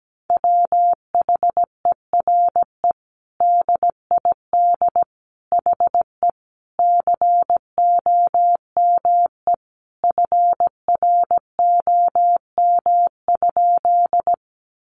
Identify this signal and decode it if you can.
Morse code, heard as a keyed tone, about 17 wpm, 700 Hz. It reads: WHERE DID HE COME FROM?
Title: Sentences from Top 200 Words 17wpm